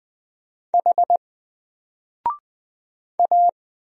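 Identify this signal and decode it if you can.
Morse code, heard as a keyed tone, about 20 wpm, 700 Hz. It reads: H A
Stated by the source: Single letter and number Morse drill at 20 wpm, from N4LCD.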